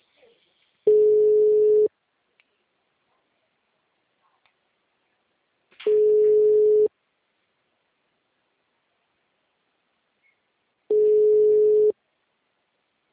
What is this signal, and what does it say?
Ringback tone (Europe, 425 Hz).